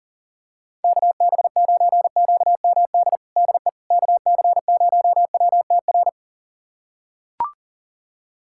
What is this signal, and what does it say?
Morse code, heard as a keyed tone, about 40 wpm, 700 Hz. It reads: KB9QMD DE KC0WTR
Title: USA Callsign DE Callsign 40wpm